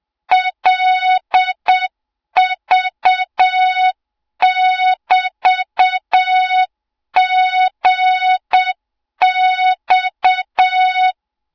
Morse code, heard as a keyed tone, about 7 wpm, 3700 Hz, with noise. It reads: LV<BT>GX
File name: Morze-HA2MN / hang / 20 GYAKORLAT